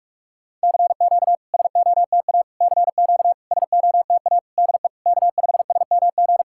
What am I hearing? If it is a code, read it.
Morse code, heard as a keyed tone, about 44 wpm, 700 Hz. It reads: CQ SOTA CQ SOTA DE K5SMG
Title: SOTA QSO Practice 45wpm